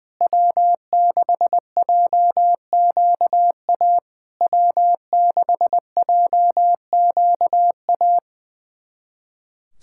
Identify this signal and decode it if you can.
Morse code, heard as a keyed tone, about 20 wpm, 700 Hz. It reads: W6JQA W6JQA
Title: SOTA QSO Practice 20wpm